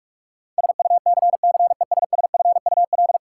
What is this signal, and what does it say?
Morse code, heard as a keyed tone, about 43 wpm, 700 Hz. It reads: SUCCESSFUL
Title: Sentences from Top 900 Words - New Word Drill - 45wpm